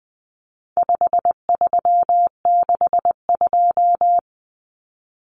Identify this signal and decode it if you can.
Morse code, heard as a keyed tone, about 20 wpm, 700 Hz. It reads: 5362
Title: Common Cut Numbers - 4-Digit 20wpm